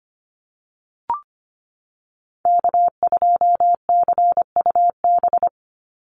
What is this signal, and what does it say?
Morse code, heard as a keyed tone, about 25 wpm, 700 Hz. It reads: K2CUB
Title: Call Signs - 25wpm